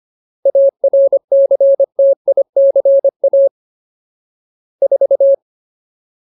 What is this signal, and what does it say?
Morse code, heard as a keyed tone, about 25 wpm, 550 Hz. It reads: ARCTICA 4
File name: Ultimate Word List / 25WPM Ultimate